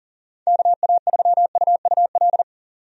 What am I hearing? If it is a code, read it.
Morse code, heard as a keyed tone, about 40 wpm, 700 Hz. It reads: KA3UUL
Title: USA Call Signs - 6 Characters - 40wpm